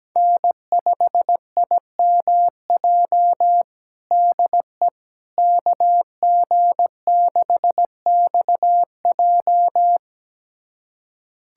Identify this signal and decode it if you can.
Morse code, heard as a keyed tone, about 17 wpm, 700 Hz. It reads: N5IMJ DE KG6XJ
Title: USA Callsign DE Callsign - 5 Characters - 17wpm